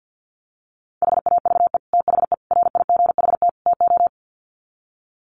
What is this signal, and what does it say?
Morse code, heard as a keyed tone, about 50 wpm, 700 Hz. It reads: HAVE THE RIGHT TO
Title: Sets of 4 Words 50wpm